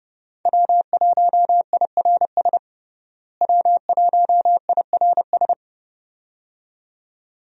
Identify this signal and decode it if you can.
Morse code, heard as a keyed tone, about 30 wpm, 700 Hz. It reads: W1IRS W1IRS
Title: USA Call Signs - 5 Characters - Repeated 2x - 30wpm